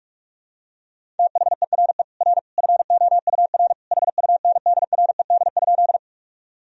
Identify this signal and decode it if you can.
Morse code, heard as a keyed tone, about 46 wpm, 700 Hz. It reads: THERE R FOUR HUNDRED?
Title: Sentences from Top 400 Words and Abbreviations 45wpm